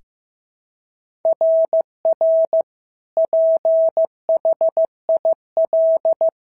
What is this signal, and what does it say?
Morse code, heard as a keyed tone, about 14 wpm, 650 Hz. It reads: RR PHIL
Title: SOTA QSO Practice 15wpm